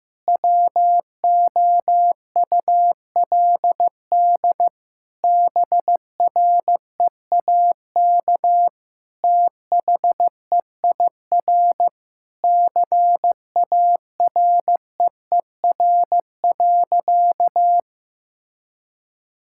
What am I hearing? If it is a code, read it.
Morse code, heard as a keyed tone, about 15 wpm, 700 Hz. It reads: WOULD BREAK THEIR CAREER.
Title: Sentences from Top 1000 Words - Review Part 6 - 15wpm